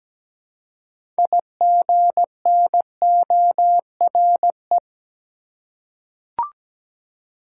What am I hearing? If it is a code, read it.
Morse code, heard as a keyed tone, about 17 wpm, 700 Hz. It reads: IGNORE
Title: Six Letter Words - 17wpm